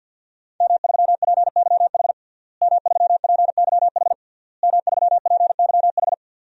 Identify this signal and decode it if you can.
Morse code, heard as a keyed tone, about 51 wpm, 700 Hz. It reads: M3PYH M3PYH M3PYH
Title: DXCC-100 Call Signs - 5 Characters - Repeated 3x - 50wpm